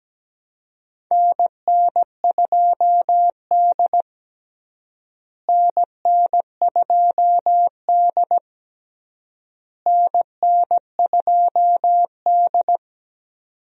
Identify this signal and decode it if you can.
Morse code, heard as a keyed tone, about 17 wpm, 700 Hz. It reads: NN2D NN2D NN2D
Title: USA Call Signs - 4 Characters - Repeated 3x - 17wpm